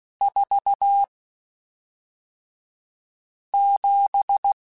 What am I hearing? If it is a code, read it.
Morse code, heard as a keyed tone, about 16 wpm, spaced as for 3 wpm, 800 Hz. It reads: 47
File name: Spaced Characters / Spaced Characters-mp3s / Weather5 5 sp @5